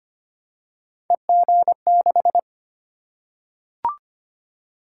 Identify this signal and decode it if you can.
Morse code, heard as a keyed tone, about 25 wpm, 700 Hz. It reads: EG6
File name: Three Letter-Number Combinations - 25wpm